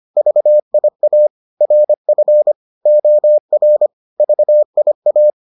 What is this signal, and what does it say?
Morse code, heard as a keyed tone, about 25 wpm, 600 Hz. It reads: VIA RF OR VIA